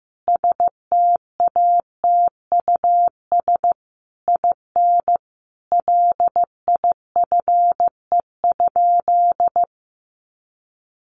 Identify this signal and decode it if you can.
Morse code, heard as a keyed tone, about 15 wpm, 700 Hz. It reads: STATUS IN LIFE?
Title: Sentences from Top 1400 Words 15wpm